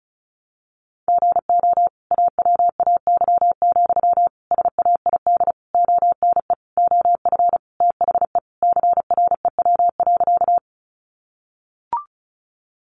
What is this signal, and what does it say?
Morse code, heard as a keyed tone, about 34 wpm, 700 Hz. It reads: GO AWAY, SAID ONE OF THE CREW.
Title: Sentences from Top 1500 Words 35wpm